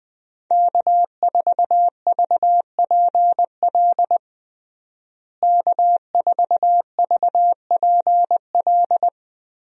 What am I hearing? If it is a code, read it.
Morse code, heard as a keyed tone, about 21 wpm, 700 Hz. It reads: K4VPL K4VPL
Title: USA Call Signs - 5 Characters - Repeated 3x - 20wpm